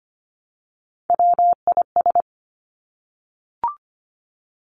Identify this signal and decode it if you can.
Morse code, heard as a keyed tone, about 25 wpm, 700 Hz. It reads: WIS
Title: Three Challenging Letter-Number Combinations 25wpm